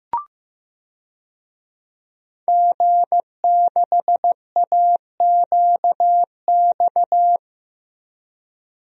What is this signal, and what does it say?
Morse code, heard as a keyed tone, about 15 wpm, 700 Hz. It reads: G6AQX
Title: DXCC-100 Call Signs - 5 Characters - Repeated 3x - 15wpm